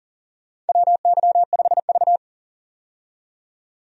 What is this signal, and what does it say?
Morse code, heard as a keyed tone, about 37 wpm, 700 Hz. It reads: WY5V
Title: USA Call Signs - 4 Characters - 40wpm